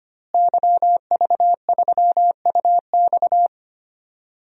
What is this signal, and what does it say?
Morse code, heard as a keyed tone, about 25 wpm, 700 Hz. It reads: YV3UX
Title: DXCC-100 Call Signs - 5 Characters - Repeated 3x - 25wpm